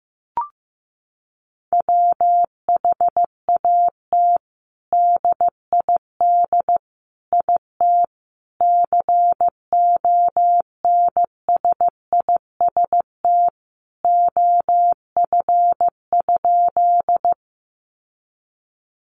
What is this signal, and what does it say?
Morse code, heard as a keyed tone, about 15 wpm, 700 Hz. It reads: WHAT DID IT CONSIST OF?
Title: Sentences from Top 1400 Words 15wpm